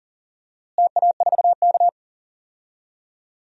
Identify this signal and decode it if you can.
Morse code, heard as a keyed tone, about 40 wpm, 700 Hz. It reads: TA4K
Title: DXCC-100 Call Signs - 4 Characters - Repeated 2x - 40wpm